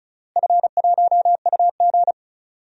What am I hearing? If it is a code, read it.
Morse code, heard as a keyed tone, about 36 wpm, 700 Hz. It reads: F1UG
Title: DXCC-100 Call Signs - 4 Characters - Repeated 3x - 35wpm